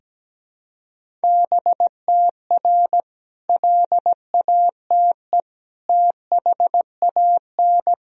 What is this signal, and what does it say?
Morse code, heard as a keyed tone, about 17 wpm, 700 Hz. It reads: BTR LATE THAN
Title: Sentences from Top 400 Words and Abbreviations 17wpm